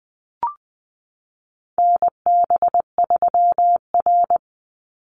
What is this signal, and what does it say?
Morse code, heard as a keyed tone, about 20 wpm, 700 Hz. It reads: NB3R
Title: Call Signs Repeated 3x - 20wpm